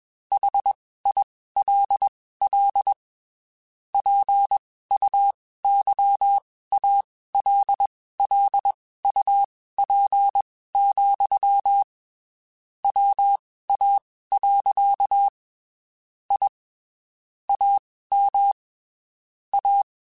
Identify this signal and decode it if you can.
Morse code, heard as a keyed tone, about 21 wpm, spaced as for 15 wpm, 800 Hz. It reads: HILL PUYALLUP, WA. I AM A
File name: (spaced) / QSOs Spaced-mp3s / kc7haa @20 sp